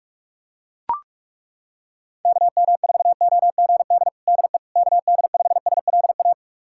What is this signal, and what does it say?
Morse code, heard as a keyed tone, about 41 wpm, 700 Hz. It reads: KM4OGD DE KD5SLA
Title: USA Callsign DE Callsign - 6 Characters - 45wpm